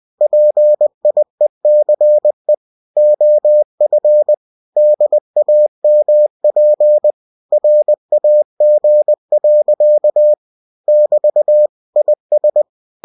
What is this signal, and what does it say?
Morse code, heard as a keyed tone, about 20 wpm, 600 Hz. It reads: PIECE OF DAMP RAG. <BT> IS
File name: cw-books / winnie-the-pooh / winnie-20wpm-page-0056